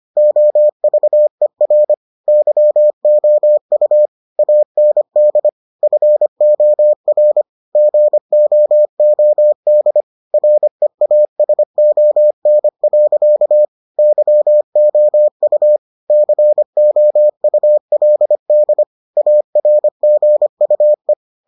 Morse code, heard as a keyed tone, about 25 wpm, 600 Hz. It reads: OVER YOU AND FOR GOOD REASON. YOU COULD ARGUE